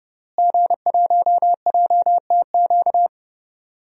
Morse code, heard as a keyed tone, about 30 wpm, 700 Hz. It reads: G1JTQ
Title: DXCC-100 Call Signs - 5 Characters - Repeated 3x - 30wpm